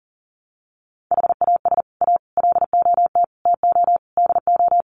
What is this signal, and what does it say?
Morse code, heard as a keyed tone, about 40 wpm, 700 Hz. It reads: HAS A LOT TO DO
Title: Sets of 5 Words Encore2 40wpm